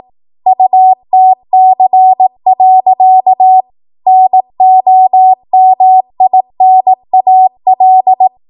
Morse code, heard as a keyed tone, about 18 wpm, 750 Hz. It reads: UTC. NOMINAL
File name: Morse / 18WPM